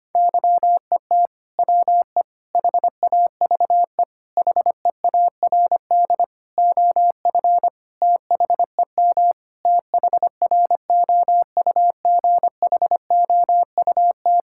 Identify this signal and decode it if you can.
Morse code, heard as a keyed tone, about 25 wpm, 700 Hz. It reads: YET WE HAVE HEARD OF THEM THROUGHOUT
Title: Sentences from Top 1000 Words - Review Part 5 - 25wpm